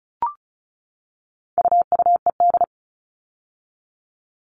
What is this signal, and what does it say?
Morse code, heard as a keyed tone, about 34 wpm, 700 Hz. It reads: UUED
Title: All Cut Numbers - 4-Digit 35wpm